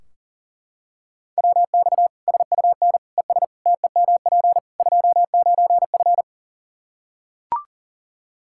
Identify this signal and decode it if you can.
Morse code, heard as a keyed tone, about 40 wpm, 700 Hz. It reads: WX SUN ES TEMP 29F
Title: K1USN SST QSO Practice 40wpm